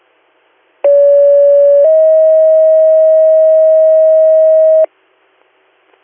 Two-tone page, heard as a one-second tone, then a three-second tone, about 570 then 640 Hz. It is heard from 0.8 to 4.8 s.